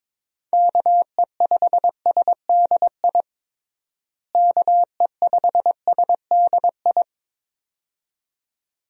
Morse code, heard as a keyed tone, about 22 wpm, 700 Hz. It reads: KE5SDI KE5SDI
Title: USA Call Signs - 6 Characters - Repeated 2x - 22wpm